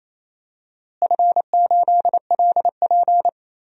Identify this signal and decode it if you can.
Morse code, heard as a keyed tone, about 28 wpm, 700 Hz. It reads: F8LP
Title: DXCC-100 Call Signs - 4 Characters - Repeated 3x - 28wpm